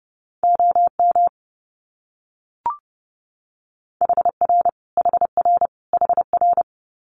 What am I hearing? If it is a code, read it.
Morse code, heard as a keyed tone, about 30 wpm, 700 Hz. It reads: OM HR HR HR
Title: Abbreviations Repeated 3x - 30wpm